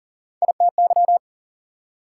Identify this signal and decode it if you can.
Morse code, heard as a keyed tone, about 53 wpm, 700 Hz. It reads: ITY